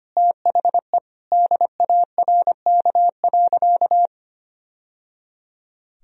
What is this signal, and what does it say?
Morse code, heard as a keyed tone, about 25 wpm, 700 Hz. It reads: THE DARK.
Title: Sentences from Top 1000 Words - Review Part 2 - 25wpm